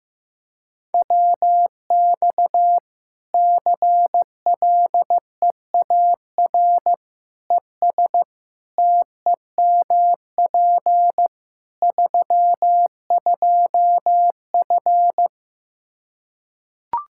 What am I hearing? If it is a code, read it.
Morse code, heard as a keyed tone, about 15 wpm, 700 Hz. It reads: WX CLEAR ES TEMP 32F
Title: Practice QSO - Exchange 2 - Info - 15wpm